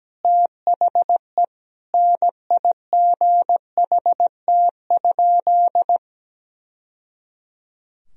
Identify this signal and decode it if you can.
Morse code, heard as a keyed tone, about 17 wpm, 700 Hz. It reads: THE NIGHT?
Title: Sentences from Top 1000 Words - Review Part 2 - 17wpm